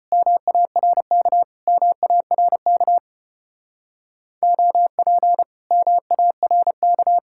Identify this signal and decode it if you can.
Morse code, heard as a keyed tone, about 32 wpm, 700 Hz. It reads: MARK MARK OP MARK